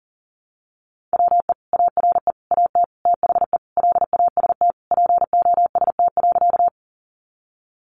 Morse code, heard as a keyed tone, about 40 wpm, 700 Hz. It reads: WE ARE AT THE LAST POST.
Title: Sentences from Top 1000 Words 40wpm